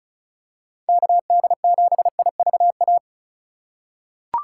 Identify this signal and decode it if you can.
Morse code, heard as a keyed tone, about 36 wpm, 700 Hz. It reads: KD7IVA
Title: USA Call Signs - 6 Characters - 2x Character Spacing - 35wpm